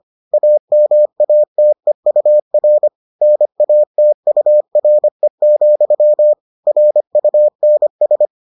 Morse code, heard as a keyed tone, about 25 wpm, 600 Hz. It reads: AMATEUR NATURE, RUNS